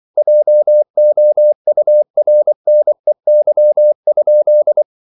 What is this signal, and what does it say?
Morse code, heard as a keyed tone, about 24 wpm, 600 Hz. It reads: JOURNEY?